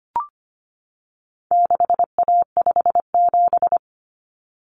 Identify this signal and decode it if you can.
Morse code, heard as a keyed tone, about 25 wpm, 700 Hz. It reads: 6A57
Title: Common Cut Numbers - 4-Digit 25wpm